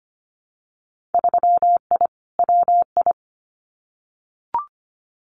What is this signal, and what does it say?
Morse code, heard as a keyed tone, about 25 wpm, 700 Hz. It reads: 3I WI
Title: Winter Field Day Exchanges 25wpm